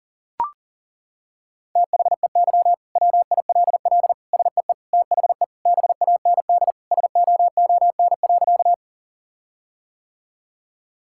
Morse code, heard as a keyed tone, about 37 wpm, 700 Hz. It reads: THEY WILL SEE THE BAND SOON.
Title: Sentences from Top 1600 Words 40wpm